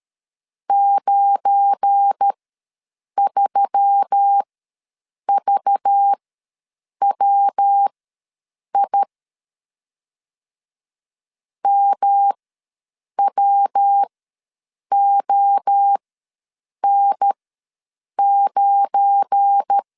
Morse code, heard as a keyed tone, about 13 wpm, spaced as for 7 wpm, 800 Hz. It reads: 93VWI MWON9
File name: LEZIONI / 7 GF9Z / 7 GF9 ws 12wpm 3344